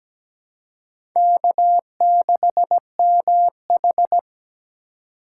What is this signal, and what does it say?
Morse code, heard as a keyed tone, about 17 wpm, 700 Hz. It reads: K6MH
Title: USA Call Signs - 4 Characters - 3x Character Spacing - 17wpm